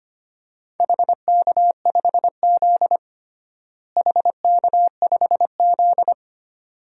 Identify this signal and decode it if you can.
Morse code, heard as a keyed tone, about 25 wpm, 700 Hz. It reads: HK5Z HK5Z
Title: DXCC-100 Call Signs - 4 Characters - Repeated 2x - 25wpm